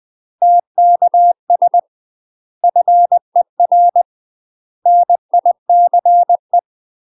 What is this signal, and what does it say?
Morse code, heard as a keyed tone, about 20 wpm, 700 Hz. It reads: TKS FER NICE
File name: UK MASTER QSOs TRAINING / 20WPM / UK MASTER QSOs 0053